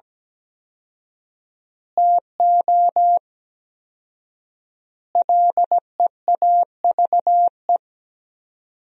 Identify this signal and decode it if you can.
Morse code, heard as a keyed tone, about 17 wpm, 700 Hz. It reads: TO LEAVE
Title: 4x Word-Spacing - Sentences from Top 400 Words - 17wpm